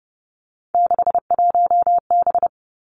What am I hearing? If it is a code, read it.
Morse code, heard as a keyed tone, about 30 wpm, 700 Hz. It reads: 61B
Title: Three Challenging Letter-Number Combinations 30wpm